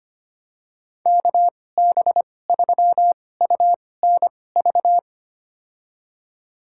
Morse code, heard as a keyed tone, about 25 wpm, spaced as for 18 wpm, 700 Hz. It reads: KB3UNV
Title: USA Call Signs - 6 Characters - 2x Character Spacing - 25wpm